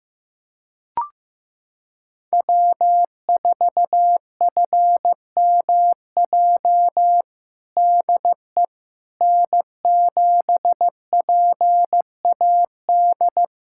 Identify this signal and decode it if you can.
Morse code, heard as a keyed tone, about 15 wpm, 700 Hz. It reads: W4FMJ DE N7PAD
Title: USA Callsign DE Callsign - 5 Characters - 15wpm